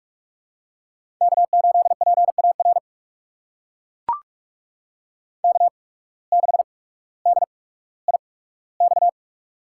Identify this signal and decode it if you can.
Morse code, heard as a keyed tone, about 44 wpm, spaced as for 19 wpm, 700 Hz. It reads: K8PAR K6DIX
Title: USA Call Signs - 5 Characters - 8x Character Spacing - 45wpm